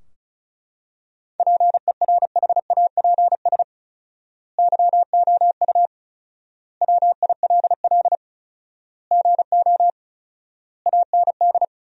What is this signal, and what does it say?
Morse code, heard as a keyed tone, about 36 wpm, 700 Hz. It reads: PERHAPS YOU WILL GO AND
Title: 4x Word-Spacing - Sentences from Top 500 Words - 35wpm